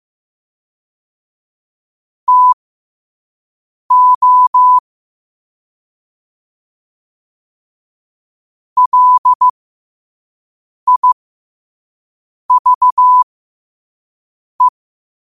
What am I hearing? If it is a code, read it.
Morse code, heard as a keyed tone, about 15 wpm, spaced as for 5 wpm, 1000 Hz. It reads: TO LIVE
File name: qotdfiles / 5wpm